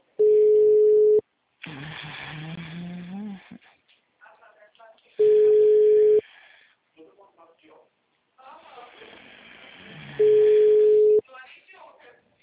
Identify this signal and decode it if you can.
Ringback tone (Europe, 425 Hz), with speech.